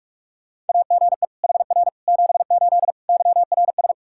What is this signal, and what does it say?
Morse code, heard as a keyed tone, about 47 wpm, 700 Hz. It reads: AGE HR 78 YRS
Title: K1USN SST QSO Practice 45wpm